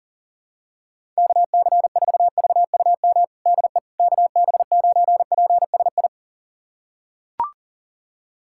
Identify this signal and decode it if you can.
Morse code, heard as a keyed tone, about 40 wpm, 700 Hz. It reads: KC4VUM DE KB9PSI
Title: USA Callsign DE Callsign - 6 Characters - 40wpm